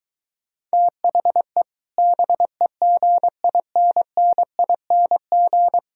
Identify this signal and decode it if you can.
Morse code, heard as a keyed tone, about 23 wpm, 700 Hz. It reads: THE BEGINNING